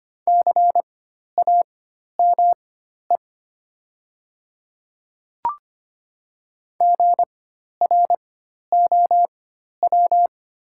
Morse code, heard as a keyed tone, about 25 wpm, spaced as for 12 wpm, 700 Hz. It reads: CAME GROW
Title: 4x Character Spacing - Top 300 Words - 25wpm